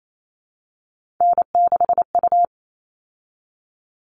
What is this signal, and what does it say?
Morse code, heard as a keyed tone, about 28 wpm, 700 Hz. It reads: N6U
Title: All Cut Numbers - 3-Digit 28wpm